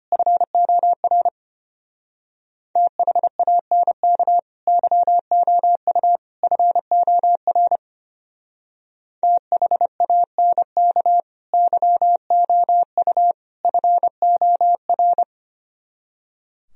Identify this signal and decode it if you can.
Morse code, heard as a keyed tone, about 31 wpm, 700 Hz. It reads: FOR THANK YOU FOR THANK YOU FOR
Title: Speed-Racing - Sets of 3 Words 25wpm